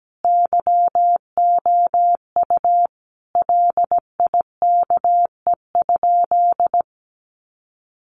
Morse code, heard as a keyed tone, about 17 wpm, 700 Hz. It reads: YOU LIKE?
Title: Sentences from Top 1200 Words 17wpm